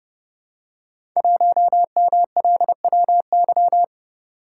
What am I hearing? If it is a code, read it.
Morse code, heard as a keyed tone, about 30 wpm, 700 Hz. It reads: 1MLWY
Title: Code Groups - 5 Letter:Numbers - 30wpm